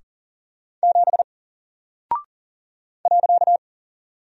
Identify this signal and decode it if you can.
Morse code, heard as a keyed tone, about 40 wpm, 700 Hz. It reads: 7 .
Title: Single Character (Letter:Number:Punctuation:Prosign) Alt 40wpm